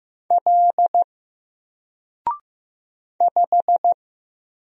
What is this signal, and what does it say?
Morse code, heard as a keyed tone, about 15 wpm, 700 Hz. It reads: L 5
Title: Single Letter-Number 15wpm